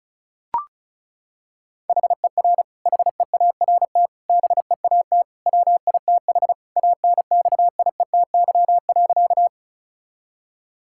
Morse code, heard as a keyed tone, about 34 wpm, 700 Hz. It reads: HER HEART BEAT WITH ANXIETY.
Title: Sentences from Top 1600 Words 35wpm